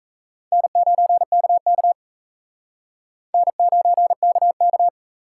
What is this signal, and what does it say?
Morse code, heard as a keyed tone, about 34 wpm, 700 Hz. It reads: N9KK N9KK
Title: Speed Racing - USA 1x2 Call Signs 28wpm